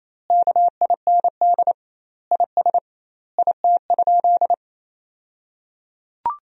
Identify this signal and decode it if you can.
Morse code, heard as a keyed tone, about 28 wpm, 700 Hz. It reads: KIND IS IT?
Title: 2x Word-Spacing - Sentences from Top 200 Words - 28wpm